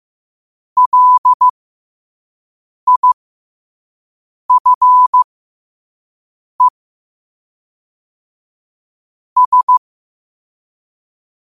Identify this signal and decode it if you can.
Morse code, heard as a keyed tone, about 15 wpm, spaced as for 5 wpm, 1000 Hz. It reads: LIFE S
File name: qotdfiles / 5wpm